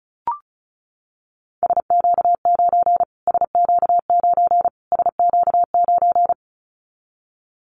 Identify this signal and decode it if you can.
Morse code, heard as a keyed tone, about 34 wpm, 700 Hz. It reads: SQ9 SQ9 SQ9
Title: Callsign Prefixes for Top 25 DXCC - Repeated 3x - 35wpm